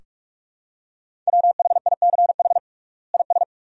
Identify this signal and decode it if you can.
Morse code, heard as a keyed tone, about 46 wpm, 700 Hz. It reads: WHICH IS
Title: 3x Word-Spacing - Sentences from Top 100 Words - 45wpm